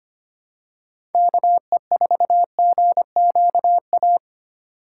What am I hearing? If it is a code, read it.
Morse code, heard as a keyed tone, about 30 wpm, 700 Hz. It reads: KE4GQA
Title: Speed Racing - USA 2x3 Call Signs 25wpm